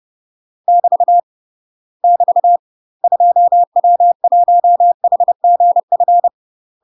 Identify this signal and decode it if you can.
Morse code, heard as a keyed tone, about 30 wpm, 700 Hz. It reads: <BT> <BT> 2W1HGF
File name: UK MASTER QSOs TRAINING / 30WPM / UK MASTER QSOs 0028